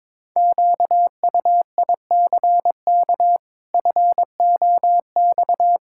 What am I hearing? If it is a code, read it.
Morse code, heard as a keyed tone, about 22 wpm, 700 Hz. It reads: QUICK FOX
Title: Pangram Challenge v2 - 22wpm